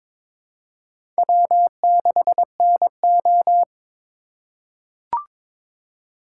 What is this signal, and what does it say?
Morse code, heard as a keyed tone, about 22 wpm, 700 Hz. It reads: W6NO
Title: USA Call Signs - 4 Characters - 2x Character Spacing - 22wpm